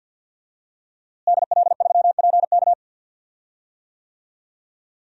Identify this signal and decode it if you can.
Morse code, heard as a keyed tone, about 52 wpm, 700 Hz. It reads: DL3PK
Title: DXCC-100 Call Signs - 5 Characters - Repeated 2x - 50wpm